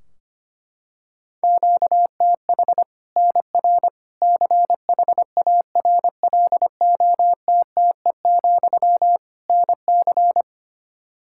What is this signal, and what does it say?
Morse code, heard as a keyed tone, about 25 wpm, 700 Hz. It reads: QTH NR CHARLOTTE, NC
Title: Practice QSO - Exchange 1 - Beginning - 25wpm